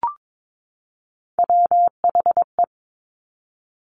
Morse code, heard as a keyed tone, about 22 wpm, 700 Hz. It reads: WHE
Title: Top Three Letter Combinations in Words 22wpm